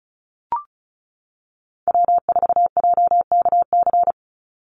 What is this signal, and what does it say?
Morse code, heard as a keyed tone, about 36 wpm, 700 Hz. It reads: W4JKC
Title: NA SOTA Chaser Callsigns 35wpm